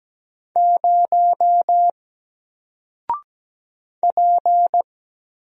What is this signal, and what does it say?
Morse code, heard as a keyed tone, about 17 wpm, 700 Hz. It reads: 0 P
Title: Single Character (Letter:Number:Punctuation:Prosign) Alt 17wpm